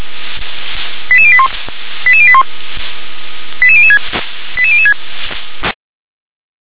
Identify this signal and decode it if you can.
Five-tone selective call (ZVEI), heard as quick runs of five tones, tones of about 70 ms. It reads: 88081 88081 88006 88006